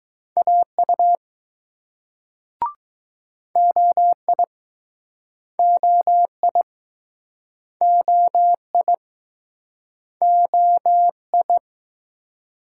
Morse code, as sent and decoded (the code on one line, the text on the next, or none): .- ..- / --- .. / --- .. / --- .. / --- ..
AU OI OI OI OI